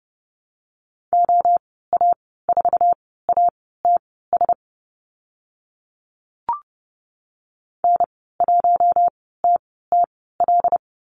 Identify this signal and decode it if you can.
Morse code, heard as a keyed tone, about 30 wpm, spaced as for 17 wpm, 700 Hz. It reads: OA4ATS N1TTL
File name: Lesson 14g - Number 2 - Call Signs 30wpm